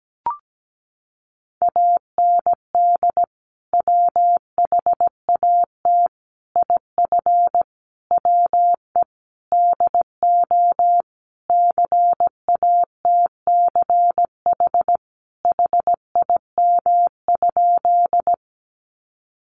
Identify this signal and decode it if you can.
Morse code, heard as a keyed tone, about 17 wpm, 700 Hz. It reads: AND WHAT IF WE DO CATCH HIM?
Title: Sentences from Top 1500 Words 17wpm